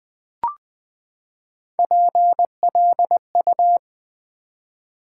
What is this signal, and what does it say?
Morse code, heard as a keyed tone, about 20 wpm, 700 Hz. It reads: PLU
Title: Build A Word - 3 Letters At A Time - 6-Letter Words - 20wpm